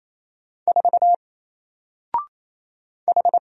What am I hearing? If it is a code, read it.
Morse code, heard as a keyed tone, about 28 wpm, 700 Hz. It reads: 4 H
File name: Single-Letter-Number-28wpm